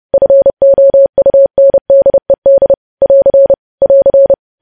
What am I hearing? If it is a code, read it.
Morse code, heard as a keyed tone, about 30 wpm, 550 Hz. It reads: FOUNDED <AR> <AR>
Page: news headlines in morse code at 30 wpm